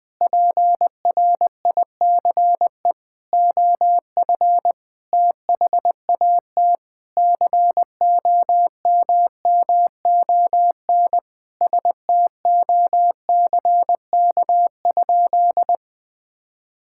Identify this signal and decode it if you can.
Morse code, heard as a keyed tone, about 20 wpm, 700 Hz. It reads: PRICE OF THAT COMMON STOCK?
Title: Sentences from Top 1000 Words - Review Part 5 - 20wpm